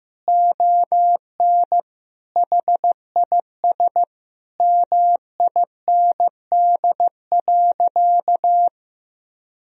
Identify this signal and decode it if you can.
Morse code, heard as a keyed tone, about 15 wpm, 700 Hz. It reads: ON HIS MIND.
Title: Sentences from Top 1000 Words - Review Part 2 - 15wpm